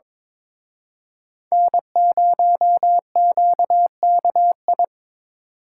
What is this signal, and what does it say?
Morse code, heard as a keyed tone, about 25 wpm, 700 Hz. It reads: N0QKI
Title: Speed Racing - USA 1x3 Call Signs 22wpm